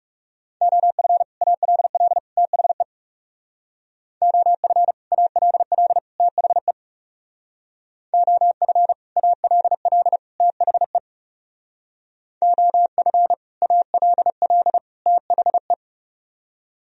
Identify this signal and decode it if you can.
Morse code, heard as a keyed tone, about 37 wpm, 700 Hz. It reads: OF ALL THE OF ALL THE OF ALL THE OF ALL THE
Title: Speed-Racing - Sets of 3 Words 30wpm